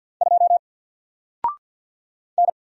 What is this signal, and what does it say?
Morse code, heard as a keyed tone, about 48 wpm, 700 Hz. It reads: 2 N
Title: Single Letter-Number Alt 50wpm